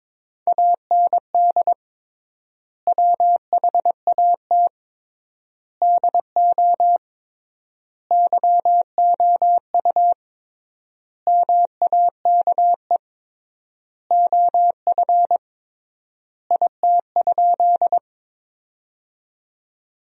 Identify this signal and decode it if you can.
Morse code, heard as a keyed tone, about 22 wpm, 700 Hz. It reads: AND WHAT DO YOU MAKE OF IT?
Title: 3x Word-Spacing - Sentences from Top 100 Words - 22wpm